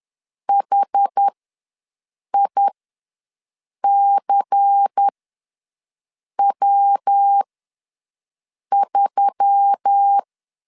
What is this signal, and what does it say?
Morse code, heard as a keyed tone, about 10 wpm, spaced as for 5 wpm, 800 Hz. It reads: HICW3
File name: LEZIONI / 5 LBVC / 5 LBVC ws 10wpm 3733